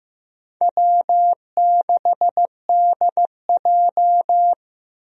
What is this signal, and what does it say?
Morse code, heard as a keyed tone, about 15 wpm, 700 Hz. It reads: W6DJ
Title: USA Call Signs - 4 Characters - Repeated 3x - 15wpm